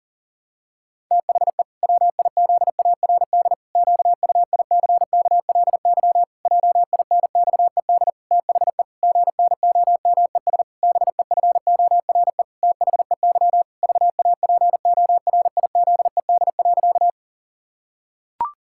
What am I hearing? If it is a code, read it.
Morse code, heard as a keyed tone, about 40 wpm, 700 Hz. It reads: THE WIZARD QUICKLY JINXED THE GNOMES BEFORE THEY VAPORIZED.
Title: Pangram Challenge v2 - 40wpm